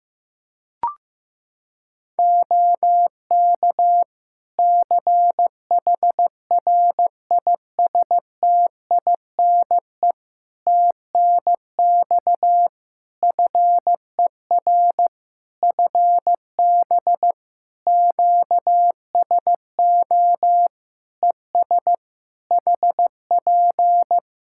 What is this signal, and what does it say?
Morse code, heard as a keyed tone, about 15 wpm, 700 Hz. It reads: OK CHRISTINE TNX FER FB QSO ES HP
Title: Practice QSO - Exchange 4 - Ending - 15wpm